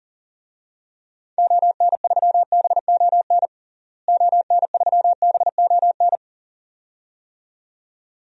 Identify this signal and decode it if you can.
Morse code, heard as a keyed tone, about 40 wpm, 700 Hz. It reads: ON3BON ON3BON
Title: DXCC-100 Call Signs - 6 Characters - Repeated 2x - 40wpm